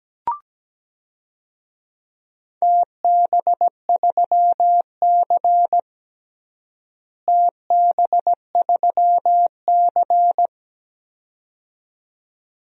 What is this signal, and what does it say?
Morse code, heard as a keyed tone, about 17 wpm, 700 Hz. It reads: TB3C TB3C
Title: DXCC-100 Call Signs - 4 Characters - Repeated 2x - 17wpm